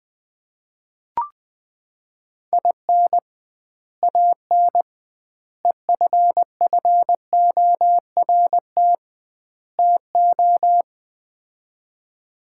Morse code, heard as a keyed tone, about 21 wpm, 700 Hz. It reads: IN AN EFFORT TO
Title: 2x Word-Spacing - Sets of 4 Words - 20wpm